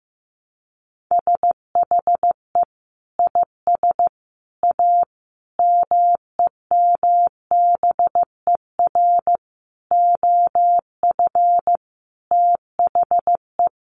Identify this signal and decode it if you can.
Morse code, heard as a keyed tone, about 14 wpm, 700 Hz. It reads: SHE IS A MEMBER OF THE
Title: Sentences from Top 1100 Words 15wpm